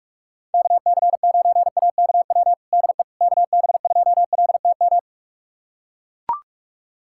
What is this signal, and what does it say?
Morse code, heard as a keyed tone, about 44 wpm, 700 Hz. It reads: KC9AKW DE KB2LTM
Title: USA Callsign DE Callsign 45wpm